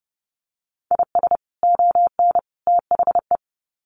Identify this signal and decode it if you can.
Morse code, heard as a keyed tone, about 30 wpm, 700 Hz. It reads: IS ON THE
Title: Sets of 3 Words 30wpm